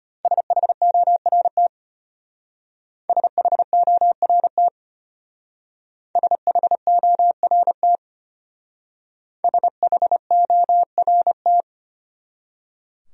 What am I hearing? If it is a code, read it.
Morse code, heard as a keyed tone, about 31 wpm, 700 Hz. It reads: SHORT SHORT SHORT SHORT
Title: Speed-Racing Top 500 Words 25wpm